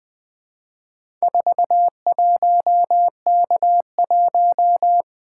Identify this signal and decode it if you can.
Morse code, heard as a keyed tone, about 20 wpm, 700 Hz. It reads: 41K1
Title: Four Letter-Number Combinations - 20wpm